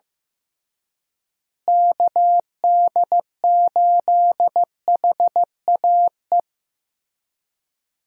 Morse code, heard as a keyed tone, about 15 wpm, 700 Hz. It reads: KD8HAE
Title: USA Call Signs - 6 Characters - Repeated 3x - 15wpm